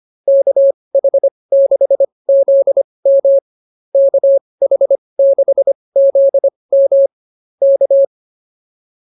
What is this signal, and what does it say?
Morse code, heard as a keyed tone, about 24 wpm, spaced as for 20 wpm, 550 Hz. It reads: KH6ZM KH6ZM K
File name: QSO109a 20